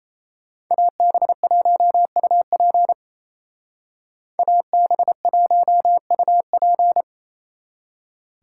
Morse code, heard as a keyed tone, about 31 wpm, 700 Hz. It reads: AB1UP AB1UP
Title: Speed Racing - USA 2x2 Call Signs 28wpm